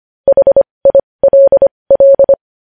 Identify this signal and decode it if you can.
Morse code, heard as a keyed tone, about 25 wpm, spaced as for 20 wpm, 550 Hz. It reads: HILL
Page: news-headlines-in-morse-code-at-20-wpm